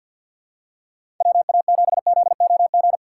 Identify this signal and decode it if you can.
Morse code, heard as a keyed tone, about 49 wpm, 700 Hz. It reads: WA7ZOG
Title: USA Call Signs - 6 Characters - 4x Character Spacing - 50wpm